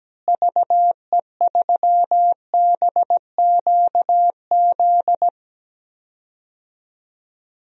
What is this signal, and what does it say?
Morse code, heard as a keyed tone, about 17 wpm, 700 Hz. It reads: VE3BQZ
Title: DXCC-100 Call Signs - 6 Characters - Repeated 3x - 17wpm